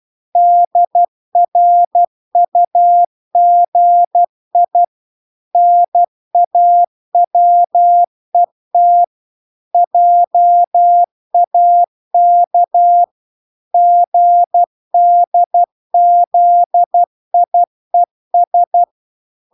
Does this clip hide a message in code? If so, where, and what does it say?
Yes: Morse code from 0.4 to 18.8 s, heard as a keyed tone, about 12 wpm, 700 Hz. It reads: DRUGI NAWET JAK GDZIES